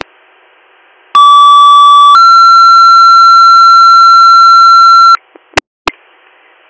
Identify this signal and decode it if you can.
Two-tone page, heard as a one-second tone, then a three-second tone, about 1150 then 1360 Hz. It is heard from 1.1 to 5.2 s.